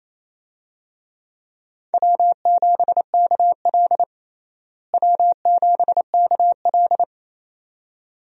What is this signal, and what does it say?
Morse code, heard as a keyed tone, about 28 wpm, 700 Hz. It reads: W7KL W7KL
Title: USA Call Signs - 4 Characters - Repeated 2x - 28wpm